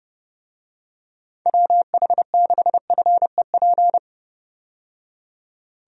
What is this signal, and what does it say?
Morse code, heard as a keyed tone, about 32 wpm, 700 Hz. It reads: WH6FEP
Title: USA Call Signs - 6 Characters - 30wpm